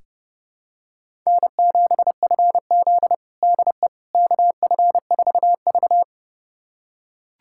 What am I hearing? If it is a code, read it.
Morse code, heard as a keyed tone, about 30 wpm, 700 Hz. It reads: N7FZ DE KF4V
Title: USA Callsign DE Callsign - 4 Characters - 30wpm